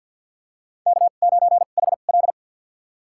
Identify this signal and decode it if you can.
Morse code, heard as a keyed tone, about 53 wpm, spaced as for 37 wpm, 700 Hz. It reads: K9HL
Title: USA Call Signs - 4 Characters - 2x Character Spacing - 50wpm